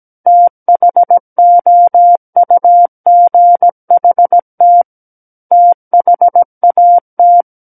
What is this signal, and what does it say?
Morse code, heard as a keyed tone, about 17 wpm, 700 Hz. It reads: THOUGHT THAT
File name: Book One / 08-Chapter 8 - 17 WPM